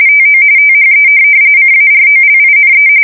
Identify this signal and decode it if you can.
RTTY, about 45.45 baud. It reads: TO THE CAROLINE